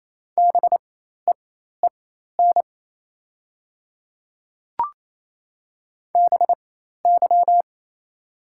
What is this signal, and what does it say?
Morse code, heard as a keyed tone, about 28 wpm, spaced as for 13 wpm, 700 Hz. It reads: BEEN BY
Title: 4x Character Spacing - Top 100 Words - 28wpm